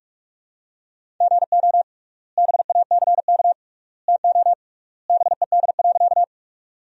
Morse code, heard as a keyed tone, about 46 wpm, 700 Hz. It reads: GO BACK TO BED.
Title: 3x Word-Spacing - Sentences from Top 500 Words - 45wpm